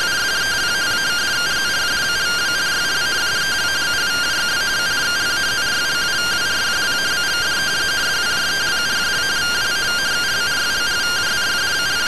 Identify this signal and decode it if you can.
Two-tone teleprinter signal, about 170 Hz shift.